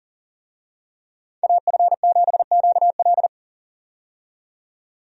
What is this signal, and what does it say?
Morse code, heard as a keyed tone, about 40 wpm, 700 Hz. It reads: AF7QL
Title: USA Call Signs - 5 Characters - 40wpm